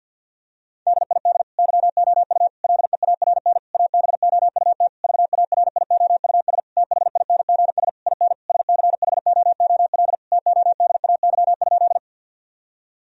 Morse code, heard as a keyed tone, about 50 wpm, 700 Hz. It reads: DID YOU LEARN ABOUT VARIOUS THINGS IN SCHOOL TODAY?
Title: Sentences from Top 1000 Words - Review Part 4 - 50wpm